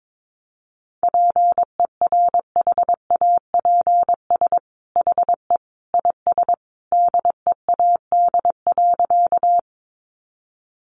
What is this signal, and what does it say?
Morse code, heard as a keyed tone, about 22 wpm, 700 Hz. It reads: PERHAPS HE IS DEAD.
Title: Sentences from Top 1500 Words 22wpm